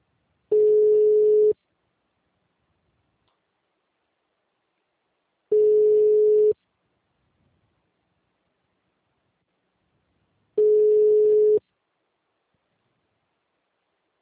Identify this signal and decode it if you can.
Ringback tone (Europe, 425 Hz).
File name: out- -2019- - - .46189